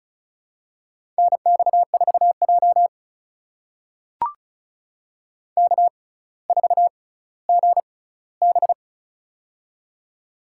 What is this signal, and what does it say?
Morse code, heard as a keyed tone, about 36 wpm, spaced as for 18 wpm, 700 Hz. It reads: NX4J K4GB
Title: USA Call Signs - 4 Characters - 6x Character Spacing - 35wpm